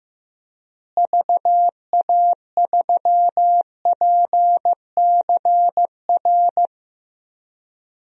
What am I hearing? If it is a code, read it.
Morse code, heard as a keyed tone, about 16 wpm, 700 Hz. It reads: VA3PCR
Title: DXCC-100 Call Signs - 6 Characters - Repeated 3x - 15wpm